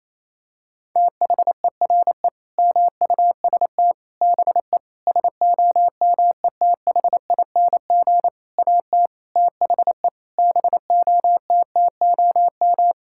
Morse code, heard as a keyed tone, about 28 wpm, 700 Hz. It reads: THERE MUST BE SOMETHING AT THE BOTTOM
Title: Sentences from Top 1000 Words - Review Part 6 - 28wpm